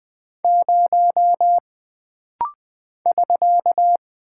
Morse code, heard as a keyed tone, about 20 wpm, 700 Hz. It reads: 0 <SK>
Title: Single Character (Letter:Number:Punctuation:Prosign) - Rapid-Fire - 20wpm